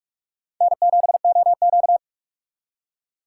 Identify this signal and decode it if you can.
Morse code, heard as a keyed tone, about 56 wpm, 700 Hz. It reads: N7OQ